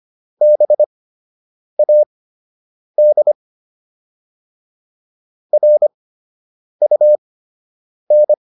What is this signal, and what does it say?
Morse code, heard as a keyed tone, about 25 wpm, spaced as for 8 wpm, 600 Hz. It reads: BAD RUN